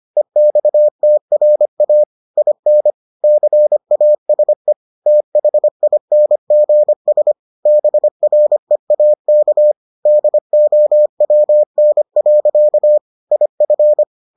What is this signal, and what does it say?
Morse code, heard as a keyed tone, about 25 wpm, 600 Hz. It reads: EXTRA IN CASE THINGS BREAK DOWN. IF